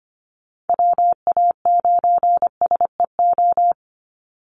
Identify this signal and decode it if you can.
Morse code, heard as a keyed tone, about 25 wpm, 700 Hz. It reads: WA9SEO
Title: Call Signs - 25wpm